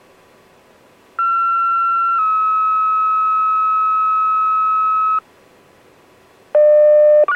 Two-tone page, heard as a one-second tone, then a three-second tone, about 1360 then 1290 Hz. It is heard from 1.2 to 5.2 s.